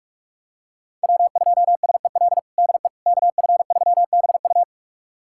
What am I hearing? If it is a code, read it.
Morse code, heard as a keyed tone, about 46 wpm, 700 Hz. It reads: W2SEL DE KF3BU
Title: USA Callsign DE Callsign - 5 Characters - 45wpm